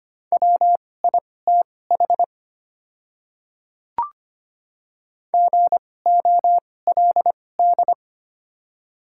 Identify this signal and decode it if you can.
Morse code, heard as a keyed tone, about 25 wpm, spaced as for 18 wpm, 700 Hz. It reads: WITH GOLD
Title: 2x Character Spacing - Top 500 Words - 25wpm